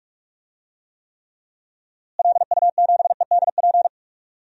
Morse code, heard as a keyed tone, about 46 wpm, 700 Hz. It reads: LU7EDP
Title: DXCC-100 Call Signs - 6 Characters - Repeated 2x - 45wpm